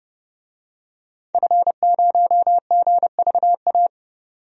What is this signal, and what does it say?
Morse code, heard as a keyed tone, about 30 wpm, 700 Hz. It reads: F0GVA
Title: DXCC-100 Call Signs - 5 Characters - 30wpm